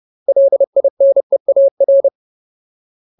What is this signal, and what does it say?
Morse code, heard as a keyed tone, about 30 wpm, 550 Hz. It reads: LINEAR
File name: Ultimate Word List / 30WPM Ultimate